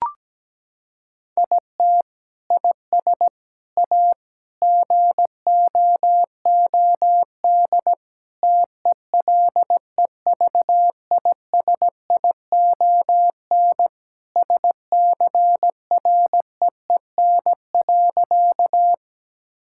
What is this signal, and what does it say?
Morse code, heard as a keyed tone, about 17 wpm, 700 Hz. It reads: IT IS A GOOD TELEVISION SCREEN.
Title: Sentences from Top 1000 Words - Review Part 6 - 17wpm